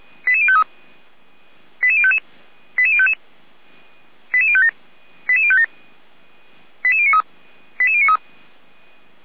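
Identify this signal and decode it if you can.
Five-tone selective call (ZVEI), heard as quick runs of five tones, tones of about 70 ms. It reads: 88053 88055 88055 88057 88057 88093 88093
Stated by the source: ZVEI five-tone selective call 88019 received by SDR, from the archive of Feuerwehr Orlamünde (fire brigade).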